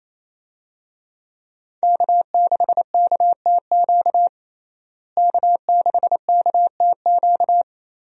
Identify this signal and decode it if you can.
Morse code, heard as a keyed tone, about 28 wpm, 700 Hz. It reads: K6KTQ K6KTQ
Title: USA Call Signs - 5 Characters - Repeated 3x - 28wpm